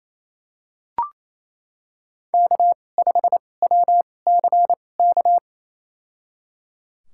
Morse code, heard as a keyed tone, about 28 wpm, spaced as for 20 wpm, 700 Hz. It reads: K5WCK
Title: USA Call Signs - 5 Characters - 2x Character Spacing - 28wpm